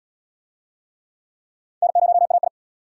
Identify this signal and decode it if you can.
Morse code, heard as a keyed tone, about 77 wpm, 700 Hz. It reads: N0AI